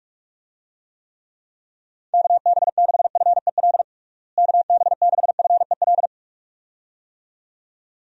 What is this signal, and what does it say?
Morse code, heard as a keyed tone, about 47 wpm, 700 Hz. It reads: KB6FEL KB6FEL
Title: USA Call Signs - 6 Characters - Repeated 2x - 45wpm